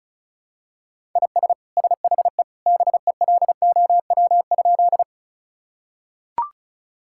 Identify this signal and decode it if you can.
Morse code, heard as a keyed tone, about 34 wpm, 700 Hz. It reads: IS SHE BELOW?
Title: Sentences from Top 1000 Words - Review Part 5 - 35wpm